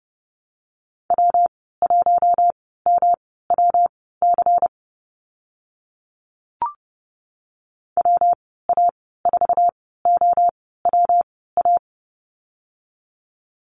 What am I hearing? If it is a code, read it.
Morse code, heard as a keyed tone, about 29 wpm, spaced as for 17 wpm, 700 Hz. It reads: W1MWC WA4OWA
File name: Lesson 19g - Letter w - Call Signs 30wpm